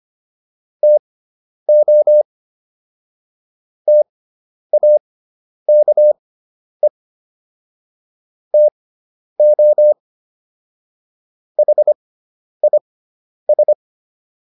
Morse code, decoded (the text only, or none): TO TAKE TO HIS